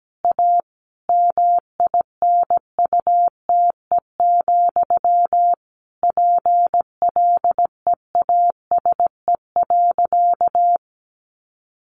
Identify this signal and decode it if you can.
Morse code, heard as a keyed tone, about 17 wpm, 700 Hz. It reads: A MINUTE, PLEASE.
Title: Sentences from Top 1500 Words 17wpm